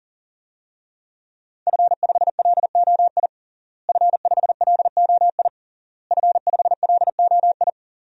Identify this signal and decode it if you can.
Morse code, heard as a keyed tone, about 40 wpm, 700 Hz. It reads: F5LOI F5LOI F5LOI
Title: DXCC-100 Call Signs - 5 Characters - Repeated 3x - 40wpm